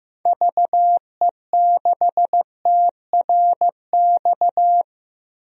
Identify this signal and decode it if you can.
Morse code, heard as a keyed tone, about 16 wpm, 700 Hz. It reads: VE6TRX
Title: DXCC-100 Call Signs - 6 Characters - Repeated 2x - 15wpm